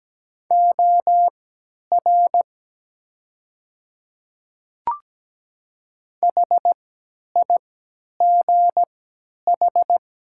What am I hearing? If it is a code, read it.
Morse code, heard as a keyed tone, about 17 wpm, spaced as for 10 wpm, 700 Hz. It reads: OR HIGH
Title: 3x Character Spacing - Top 200 Words - 17wpm